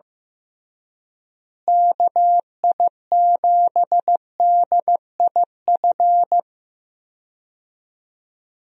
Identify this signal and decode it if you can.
Morse code, heard as a keyed tone, about 14 wpm, 700 Hz. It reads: KI7DIF
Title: USA Call Signs - 6 Characters - Repeated 3x - 15wpm